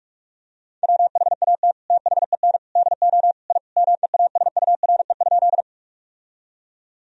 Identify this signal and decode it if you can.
Morse code, heard as a keyed tone, about 46 wpm, 700 Hz. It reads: WHAT THEN DO I MEASURE?
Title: Sentences from Top 1000 Words - Review Part 2 - 45wpm